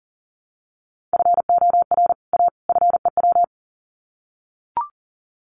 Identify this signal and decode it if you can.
Morse code, heard as a keyed tone, about 40 wpm, 700 Hz. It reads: FOR A FEW
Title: Sets of 3 Words 40wpm